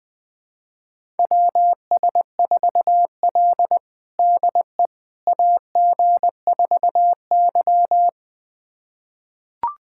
Morse code, heard as a keyed tone, about 21 wpm, 700 Hz. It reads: WS4L DE AG4Y
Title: USA Callsign DE Callsign - 4 Characters - 20wpm